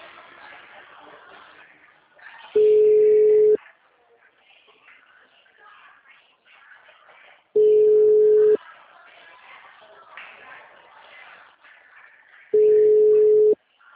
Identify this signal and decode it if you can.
Ringback tone (Europe, 425 Hz), with speech.